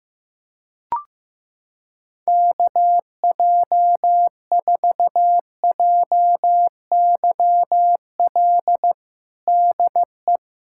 Morse code, heard as a keyed tone, about 15 wpm, 700 Hz. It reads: KJ4JYL DE
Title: USA Callsign DE Callsign 15wpm